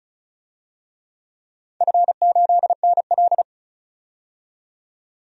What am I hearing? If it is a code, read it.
Morse code, heard as a keyed tone, about 34 wpm, 700 Hz. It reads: F8NL
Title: DXCC-100 Call Signs - 4 Characters - 35wpm